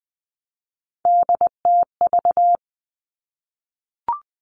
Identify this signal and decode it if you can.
Morse code, heard as a keyed tone, about 20 wpm, 700 Hz. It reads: DTV
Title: Three Challenging Letter-Number Combinations 20wpm